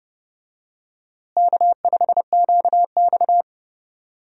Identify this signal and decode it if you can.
Morse code, heard as a keyed tone, about 32 wpm, 700 Hz. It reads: K5QX
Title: USA Call Signs - 4 Characters - Repeated 3x - 30wpm